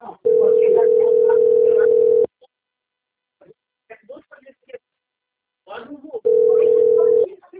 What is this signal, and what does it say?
Ringback tone (North America), 440 + 480 Hz, with speech.
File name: out- - - - - .73150